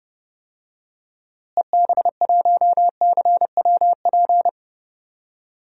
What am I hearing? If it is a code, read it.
Morse code, heard as a keyed tone, about 30 wpm, 700 Hz. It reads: EB1CWP
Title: DXCC-100 Call Signs - 6 Characters - Repeated 3x - 30wpm